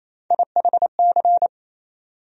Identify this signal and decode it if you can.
Morse code, heard as a keyed tone, about 28 wpm, 700 Hz. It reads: IHC
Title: USA Call Sign Postfixes - 3 Letters 28wpm